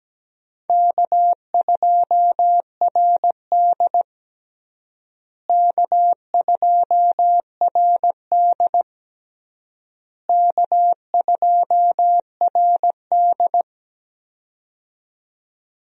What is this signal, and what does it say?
Morse code, heard as a keyed tone, about 17 wpm, 700 Hz. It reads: K2RD K2RD K2RD
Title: USA Call Signs - 4 Characters - Repeated 3x - 17wpm